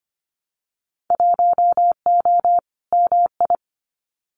Morse code, heard as a keyed tone, about 25 wpm, 700 Hz. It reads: 1O MI
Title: Winter Field Day Exchanges 25wpm